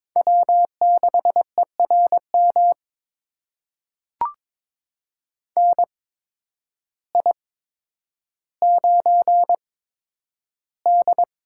Morse code, heard as a keyed tone, about 22 wpm, spaced as for 10 wpm, 700 Hz. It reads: W6ERM NI9D
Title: USA Call Signs - 8x Character Spacing - 22wpm